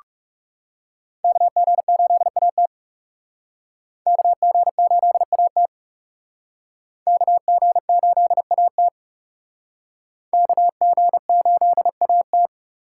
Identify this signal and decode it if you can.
Morse code, heard as a keyed tone, about 37 wpm, 700 Hz. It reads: KG8AT KG8AT KG8AT KG8AT
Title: Speed Racing - USA 2x2 Call Signs 30wpm